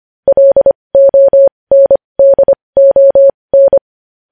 Morse code, heard as a keyed tone, about 25 wpm, spaced as for 20 wpm, 550 Hz. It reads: LONDON
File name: podcast / News headlines-1x-25x20